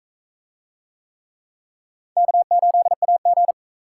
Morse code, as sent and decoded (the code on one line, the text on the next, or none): -.- ---.. .- --.
K8AG